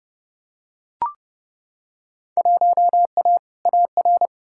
Morse code, heard as a keyed tone, about 30 wpm, 700 Hz. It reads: 1A AR
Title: Field Day Exchanges - 30wpm